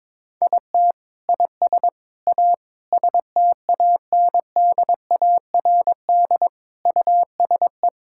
Morse code, heard as a keyed tone, about 22 wpm, 700 Hz. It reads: IT IS A STANDARD USE
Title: Sentences from Top 1000 Words - Review Part 6 - 22wpm